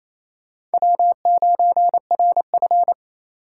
Morse code, heard as a keyed tone, about 28 wpm, 700 Hz. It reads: W9RF
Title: USA Call Signs - 4 Characters - 4x Character Spacing - 28wpm